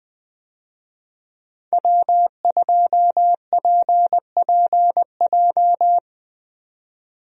Morse code, heard as a keyed tone, about 20 wpm, 700 Hz. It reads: W2PPJ
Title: USA Call Signs - 5 Characters - 20wpm